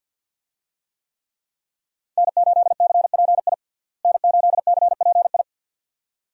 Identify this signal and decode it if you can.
Morse code, heard as a keyed tone, about 52 wpm, 700 Hz. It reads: N8CPI N8CPI
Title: USA Call Signs - 5 Characters - Repeated 2x - 50wpm